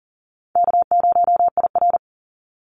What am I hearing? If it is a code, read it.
Morse code, heard as a keyed tone, about 37 wpm, 700 Hz. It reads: K0IR
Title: Call Signs - 40wpm